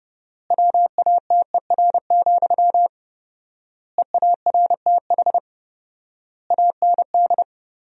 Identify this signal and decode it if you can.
Morse code, heard as a keyed tone, about 30 wpm, 700 Hz. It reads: WATER, EARTH AND
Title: 4x Word-Spacing - Sentences from Top 400 Words - 30wpm